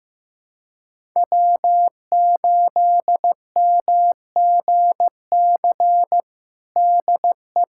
Morse code, heard as a keyed tone, about 15 wpm, 700 Hz. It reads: W8MGC DE
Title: USA Callsign DE Callsign - 5 Characters - 15wpm